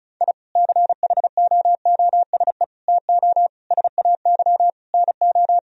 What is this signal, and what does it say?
Morse code, heard as a keyed tone, about 36 wpm, 700 Hz. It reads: I CHOOSE TO SAY NO
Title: Sentences from Top 1000 Words - Review Part 4 - 35wpm